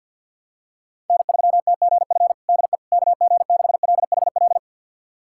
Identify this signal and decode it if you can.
Morse code, heard as a keyed tone, about 47 wpm, 700 Hz. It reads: N3TGF DE KG6LHL
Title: USA Callsign DE Callsign 50wpm